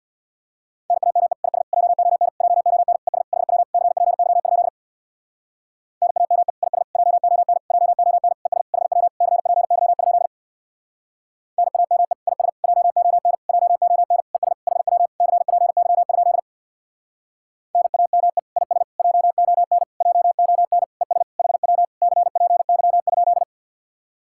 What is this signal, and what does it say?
Morse code, heard as a keyed tone, about 61 wpm, 700 Hz. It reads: NAME IS JON JON ES HW CPY? NAME IS JON JON ES HW CPY? NAME IS JON JON ES HW CPY? NAME IS JON JON ES HW CPY?
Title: Speed-Racing Common QSO Phrases - 50wpm